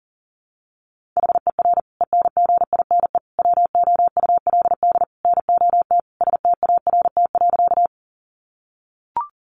Morse code, heard as a keyed tone, about 40 wpm, 700 Hz. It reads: HER ENGINE WOULD NOT START.
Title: Sentences from Top 1500 Words 40wpm